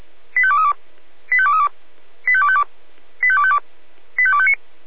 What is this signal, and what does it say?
Five-tone selective call (ZVEI), heard as quick runs of five tones, tones of about 70 ms. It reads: 85232 85232 85252 85252 85259